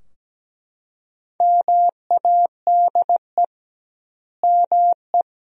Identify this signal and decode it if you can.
Morse code, heard as a keyed tone, about 17 wpm, 700 Hz. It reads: MADE ME
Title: 2x Word-Spacing - Sentences from Top 400 Words - 17wpm